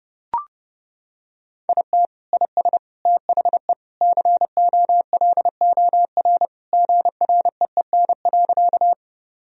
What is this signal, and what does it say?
Morse code, heard as a keyed tone, about 29 wpm, 700 Hz. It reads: IT IS THE COLOR GREEN.
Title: Sentences from Top 1000 Words - Review Part 2 - 30wpm